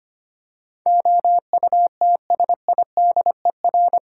Morse code, heard as a keyed tone, about 25 wpm, 700 Hz. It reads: OUTSIDER
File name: Eight Letter Words - 25wpm